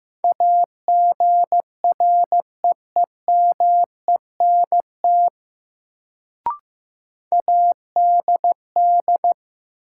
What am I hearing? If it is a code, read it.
Morse code, heard as a keyed tone, about 15 wpm, 700 Hz. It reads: AGREEMENT ADD
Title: Sentences from Top 1500 Words - New Word Drill Alt - 15wpm